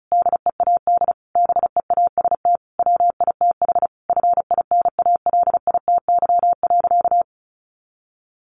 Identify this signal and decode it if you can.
Morse code, heard as a keyed tone, about 34 wpm, 700 Hz. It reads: DEAD BEAST WITH FINALITY.
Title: A Sound Of Thunder 35wpm